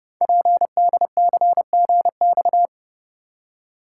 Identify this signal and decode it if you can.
Morse code, heard as a keyed tone, about 29 wpm, 700 Hz. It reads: PDCGX
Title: Code Groups - 5 Letters - 30wpm